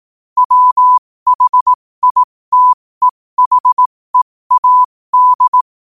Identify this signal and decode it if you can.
Morse code, heard as a keyed tone, about 18 wpm, spaced as for 15 wpm, 1000 Hz. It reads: WHITEHEAD
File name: qotdfiles / 15wpm